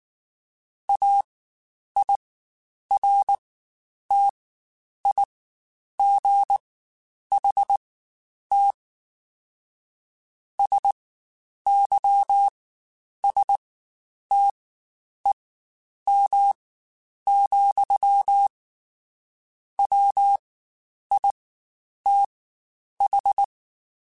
Morse code, heard as a keyed tone, about 19 wpm, spaced as for 9 wpm, 800 Hz. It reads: AIRTIGHT SYSTEM, WITH